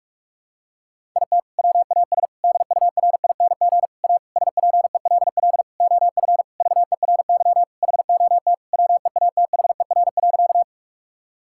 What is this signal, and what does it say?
Morse code, heard as a keyed tone, about 46 wpm, 700 Hz. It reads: IT WAS DURING A SPELL OF VERY HOT WEATHER.
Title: Sentences from Top 1000 Words - Review Part 5 - 45wpm